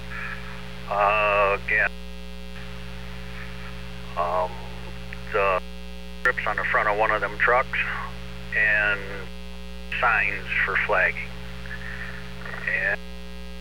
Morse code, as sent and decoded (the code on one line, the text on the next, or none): none